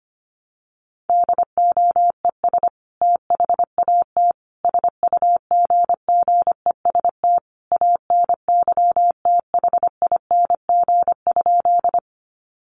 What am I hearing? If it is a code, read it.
Morse code, heard as a keyed tone, about 25 wpm, 700 Hz. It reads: DOES THAT SUGGEST ANYTHING?
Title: Sentences from Top 1300 Words 25wpm